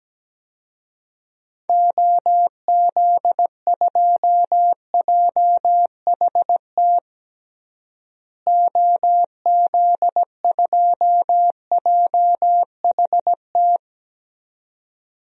Morse code, heard as a keyed tone, about 17 wpm, 700 Hz. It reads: OZ2JHT OZ2JHT
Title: DXCC-100 Call Signs - 6 Characters - Repeated 2x - 17wpm